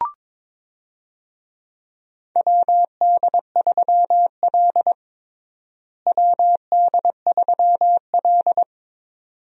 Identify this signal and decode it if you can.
Morse code, heard as a keyed tone, about 21 wpm, 700 Hz. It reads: WD3L WD3L
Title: USA Call Signs - 4 Characters - Repeated 2x - 22wpm